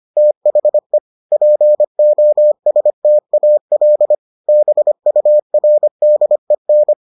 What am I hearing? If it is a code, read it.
Morse code, heard as a keyed tone, about 25 wpm, 600 Hz. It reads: THE POSTAL BURDEN